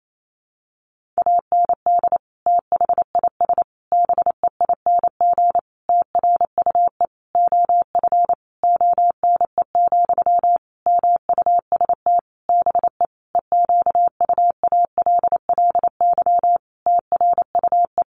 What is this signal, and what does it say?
Morse code, heard as a keyed tone, about 28 wpm, 700 Hz. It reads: AND THIS BEING TRUE OF ONE, MUST BE EQUALLY TRUE
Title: Sentences from Top 1400 Words 28wpm